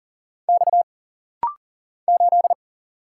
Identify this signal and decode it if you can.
Morse code, heard as a keyed tone, about 40 wpm, 700 Hz. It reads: X 8
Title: Single Letter-Number - Rapid-Fire 40wpm